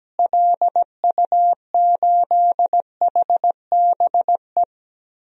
Morse code, heard as a keyed tone, about 17 wpm, 700 Hz. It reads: LU8HBE
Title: DXCC-100 Call Signs - 6 Characters - Repeated 3x - 17wpm